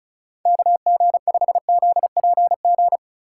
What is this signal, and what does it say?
Morse code, heard as a keyed tone, about 33 wpm, 700 Hz. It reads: KG5ZPG
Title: Speed Racing - USA 2x3 Call Signs 35wpm